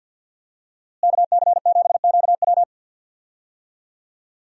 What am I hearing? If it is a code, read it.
Morse code, heard as a keyed tone, about 47 wpm, 700 Hz. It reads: KC7QW
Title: USA Call Signs - 5 Characters - Repeated 3x - 50wpm